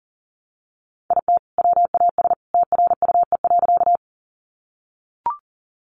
Morse code, heard as a keyed tone, about 40 wpm, 700 Hz. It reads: IT WAS TRUE.
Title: Sentences from Top 400 Words 40wpm